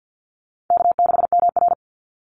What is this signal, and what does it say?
Morse code, heard as a keyed tone, about 50 wpm, 700 Hz. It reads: K6MR
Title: Call Signs - 50wpm